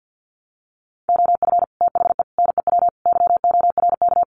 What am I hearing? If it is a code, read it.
Morse code, heard as a keyed tone, about 50 wpm, 700 Hz. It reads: OF THE NEW YORK
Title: Sets of 4 Words 50wpm